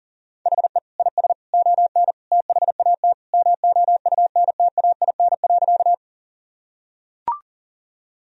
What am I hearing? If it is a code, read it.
Morse code, heard as a keyed tone, about 40 wpm, 700 Hz. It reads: HE IS ON THAT MOUNTAIN.
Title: Sentences from Top 1000 Words - Review Part 1 - 40wpm